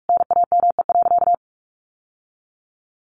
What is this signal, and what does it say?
Morse code, heard as a keyed tone, about 44 wpm, 700 Hz. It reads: NAME.
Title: Sentences from Top 1200 Words 45wpm